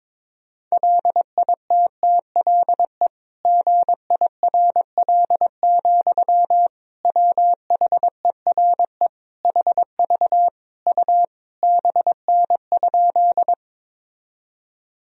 Morse code, heard as a keyed tone, about 22 wpm, 700 Hz. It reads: LITTLE GIRL, WHERE HV U BN?
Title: Sentences from Top 400 Words and Abbreviations 22wpm